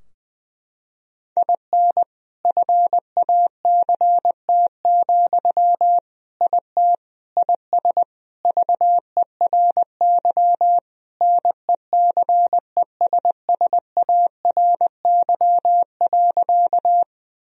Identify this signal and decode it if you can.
Morse code, heard as a keyed tone, about 20 wpm, 700 Hz. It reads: IN FACT, IT IS VERY NECESSARY.
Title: Sentences from Top 1000 Words - Review Part 3 - 20wpm